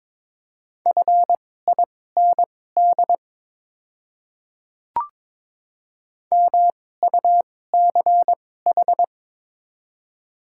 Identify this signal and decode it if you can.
Morse code, heard as a keyed tone, about 22 wpm, spaced as for 16 wpm, 700 Hz. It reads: FIND MUCH
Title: 2x Character Spacing - Top 200 Words - 22wpm